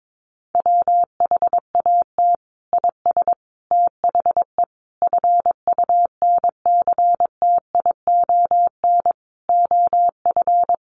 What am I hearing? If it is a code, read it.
Morse code, heard as a keyed tone, about 22 wpm, 700 Hz. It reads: WHAT IS THE FUNCTION OF
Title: Sentences from Top 1100 Words 22wpm